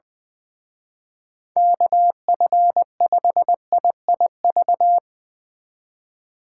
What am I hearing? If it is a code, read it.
Morse code, heard as a keyed tone, about 21 wpm, 700 Hz. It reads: KF5IIV
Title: Speed Racing - USA 2x3 Call Signs 20wpm